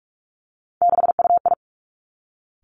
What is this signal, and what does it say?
Morse code, heard as a keyed tone, about 46 wpm, 700 Hz. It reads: 6UI
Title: Three Challenging Letter-Number Combinations 45wpm